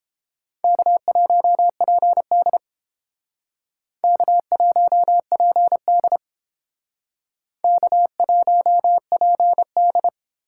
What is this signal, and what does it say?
Morse code, heard as a keyed tone, about 29 wpm, 700 Hz. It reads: K1PD K1PD K1PD